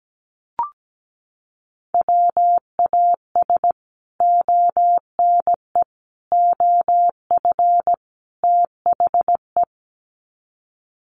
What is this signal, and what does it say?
Morse code, heard as a keyed tone, about 17 wpm, 700 Hz. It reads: WAS ONE OF THE
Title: Sets of 4 Words 17wpm